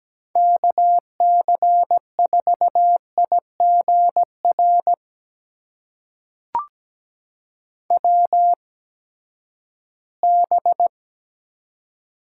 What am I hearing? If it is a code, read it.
Morse code, heard as a keyed tone, about 17 wpm, spaced as for 12 wpm, 700 Hz. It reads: KC4IGR WB
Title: USA Call Signs - 6 Characters - 8x Character Spacing - 17wpm